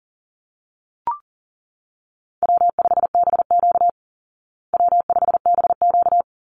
Morse code, heard as a keyed tone, about 40 wpm, 700 Hz. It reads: W5BQ W5BQ
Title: Easy Call Signs Repeated 2x - 40wpm